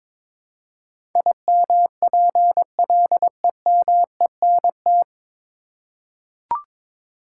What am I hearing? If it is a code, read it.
Morse code, heard as a keyed tone, about 22 wpm, 700 Hz. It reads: IMPLEMENT
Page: Morse Code Ninja practice 22wpm